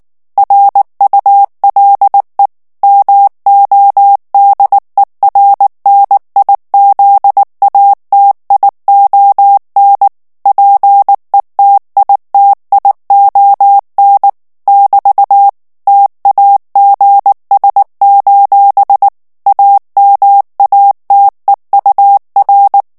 Morse code, heard as a keyed tone, about 19 wpm, 800 Hz. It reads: RULE MODERNIZATION PETITION <BT> TAGS: AMATEUR
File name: morse mp3 / ANGLAIS texte clair 3 vitesse 18 mots minute